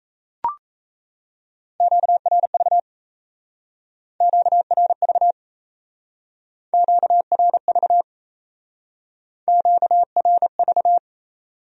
Morse code, heard as a keyed tone, about 36 wpm, 700 Hz. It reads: QRV QRV QRV QRV